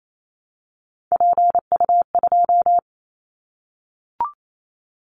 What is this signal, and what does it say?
Morse code, heard as a keyed tone, about 28 wpm, 700 Hz. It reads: PU2
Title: Callsign Prefixes for Top 50 DXCC - 28wpm